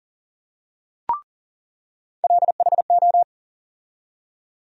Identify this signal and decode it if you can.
Morse code, heard as a keyed tone, about 40 wpm, 700 Hz. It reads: LHO
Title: USA Call Sign Postfixes - 3 Letters 40wpm